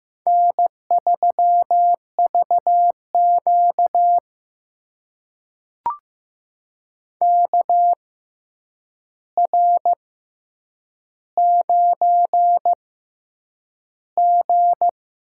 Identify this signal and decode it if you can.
Morse code, heard as a keyed tone, about 15 wpm, spaced as for 8 wpm, 700 Hz. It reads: N3VQ KR9G
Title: USA Call Signs - 4 Characters - 6x Character Spacing - 15wpm